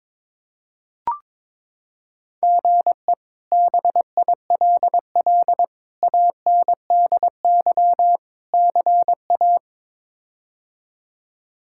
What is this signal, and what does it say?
Morse code, heard as a keyed tone, about 21 wpm, 700 Hz. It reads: GE BILL ANDY CA
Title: K1USN SST QSO Practice - 22wpm